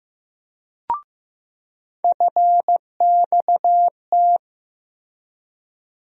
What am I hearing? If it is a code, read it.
Morse code, heard as a keyed tone, about 15 wpm, 700 Hz. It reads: FXT
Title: USA Call Sign Postfixes - 3 Letters 15wpm